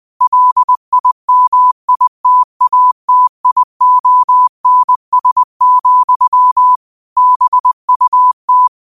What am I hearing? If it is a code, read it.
Morse code, heard as a keyed tone, about 20 wpm, 1000 Hz. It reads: LIMITATIONS, BUT